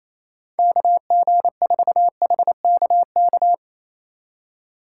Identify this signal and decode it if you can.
Morse code, heard as a keyed tone, about 35 wpm, 700 Hz. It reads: KG4HKK